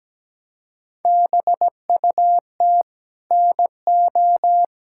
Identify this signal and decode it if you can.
Morse code, heard as a keyed tone, about 17 wpm, 700 Hz. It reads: BUT NO
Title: Sentences from Top 1600 Words 17wpm